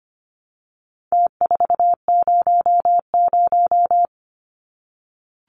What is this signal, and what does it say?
Morse code, heard as a keyed tone, about 25 wpm, 700 Hz. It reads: T400
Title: Numbers - Contest Style - 4-Digit 25wpm